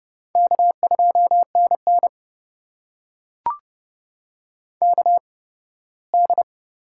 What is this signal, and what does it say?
Morse code, heard as a keyed tone, about 30 wpm, spaced as for 18 wpm, 700 Hz. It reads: K2NN KD
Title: USA Call Signs - 4 Characters - 8x Character Spacing - 30wpm